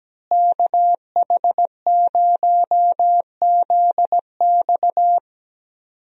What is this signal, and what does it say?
Morse code, heard as a keyed tone, about 17 wpm, 700 Hz. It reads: KH0ZX
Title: USA Call Signs - 5 Characters - Repeated 3x - 17wpm